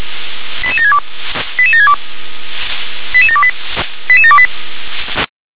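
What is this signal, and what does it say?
Five-tone selective call (ZVEI), heard as quick runs of five tones, tones of about 70 ms. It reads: 88762 88762 88728 88728
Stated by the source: ZVEI five-tone selective call 88730 received by SDR.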